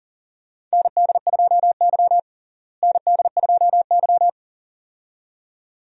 Morse code, heard as a keyed tone, about 40 wpm, 700 Hz. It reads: ND2Y ND2Y
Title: USA Call Signs - 4 Characters - Repeated 2x - 40wpm